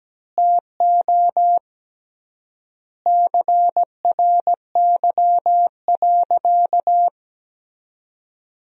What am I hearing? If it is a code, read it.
Morse code, heard as a keyed tone, about 17 wpm, 700 Hz. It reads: TO CRY.
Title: 3x Word-Spacing - Sentences from Top 500 Words - 17wpm